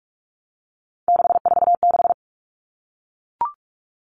Morse code, heard as a keyed tone, about 44 wpm, 700 Hz. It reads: 646
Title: Numbers - 3-Digit 45wpm